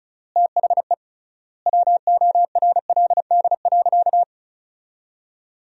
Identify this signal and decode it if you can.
Morse code, heard as a keyed tone, about 34 wpm, 700 Hz. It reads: THE WORLD.
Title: 3x Word-Spacing - Sentences from Top 200 Words - 35wpm